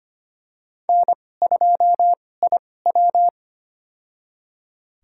Morse code, heard as a keyed tone, about 25 wpm, spaced as for 18 wpm, 700 Hz. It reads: N2IW
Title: USA Call Signs - 4 Characters - 2x Character Spacing - 25wpm